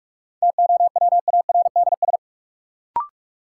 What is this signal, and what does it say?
Morse code, heard as a keyed tone, about 46 wpm, 700 Hz. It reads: TOWARDS
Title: Sentences from Top 1000 Words - Review Part 2 - New Word Drill - 45wpm